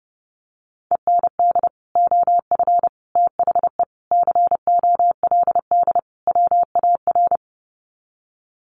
Morse code, heard as a keyed tone, about 30 wpm, 700 Hz. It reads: END OF THE COLD WAR
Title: Sets of 5 Words Encore 30wpm